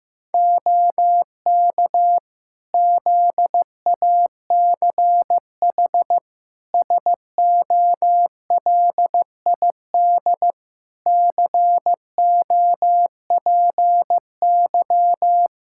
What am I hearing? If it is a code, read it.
Morse code, heard as a keyed tone, about 15 wpm, 700 Hz. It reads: OK ZACH SOLID COPY
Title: K1USN SST QSO Practice 15wpm